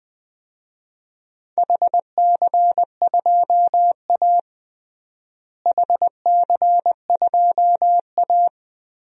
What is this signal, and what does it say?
Morse code, heard as a keyed tone, about 21 wpm, 700 Hz. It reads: HC2A HC2A
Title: DXCC-100 Call Signs - 4 Characters - Repeated 3x - 20wpm